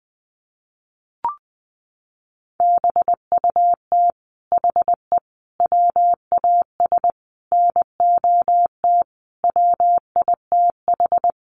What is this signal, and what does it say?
Morse code, heard as a keyed tone, about 20 wpm, 700 Hz. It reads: BUT HE WAS NOT WITH
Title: Sentences from Top 1500 Words 20wpm